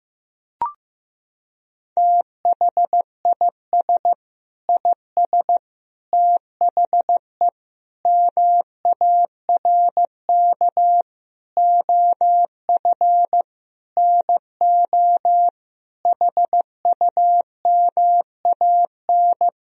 Morse code, heard as a keyed tone, about 15 wpm, 700 Hz. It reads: THIS IS THE MARK OF NO HUMAN
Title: Sentences from Top 1000 Words - Review Part 3 - 15wpm